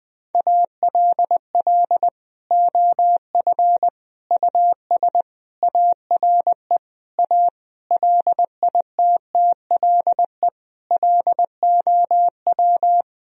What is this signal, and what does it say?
Morse code, heard as a keyed tone, about 20 wpm, 700 Hz. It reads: ALL OF US ARE A LITTLE LOW
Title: Sentences from Top 1000 Words - Review Part 3 - 20wpm